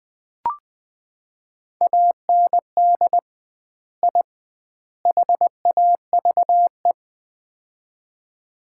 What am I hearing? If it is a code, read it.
Morse code, heard as a keyed tone, about 20 wpm, 700 Hz. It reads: AND I HAVE
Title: 2x Word-Spacing - Sets of 3 Words - 20wpm